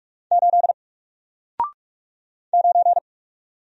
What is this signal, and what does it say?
Morse code, heard as a keyed tone, about 44 wpm, 700 Hz. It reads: 8 9
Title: Single Letter-Number Alt 45wpm